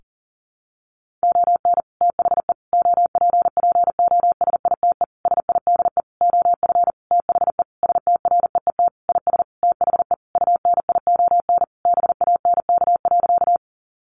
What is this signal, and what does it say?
Morse code, heard as a keyed tone, about 37 wpm, 700 Hz. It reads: ON THE OPPOSITE SIDE OF THE STREET IS THE UNION BANK.
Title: Sentences from Top 1400 Words 40wpm